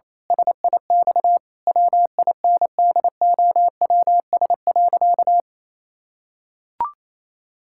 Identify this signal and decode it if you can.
Morse code, heard as a keyed tone, about 27 wpm, 700 Hz. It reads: SIX WINDOWS.
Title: Pangram Challenge v2 - 28wpm